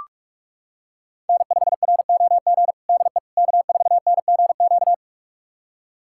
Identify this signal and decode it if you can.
Morse code, heard as a keyed tone, about 46 wpm, 700 Hz. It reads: N5ROG DE K4NGQ
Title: USA Callsign DE Callsign - 5 Characters - 45wpm